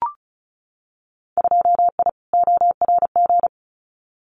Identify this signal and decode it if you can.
Morse code, heard as a keyed tone, about 34 wpm, 700 Hz. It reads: 2I ORG
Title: Winter Field Day Exchanges Encore 35wpm